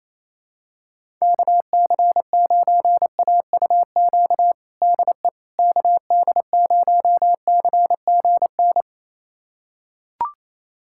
Morse code, heard as a keyed tone, about 28 wpm, 700 Hz. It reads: KC9AUQ DE KD0CGN
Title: USA Callsign DE Callsign - 6 Characters - 28wpm